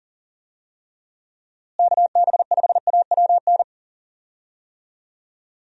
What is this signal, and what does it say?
Morse code, heard as a keyed tone, about 40 wpm, 700 Hz. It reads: KB5AWN
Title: USA Call Signs - 6 Characters - Repeated 3x - 40wpm